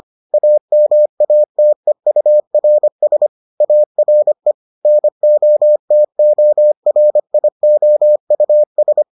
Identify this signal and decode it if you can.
Morse code, heard as a keyed tone, about 25 wpm, 600 Hz. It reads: AMATEURS ARE NOTORIOUS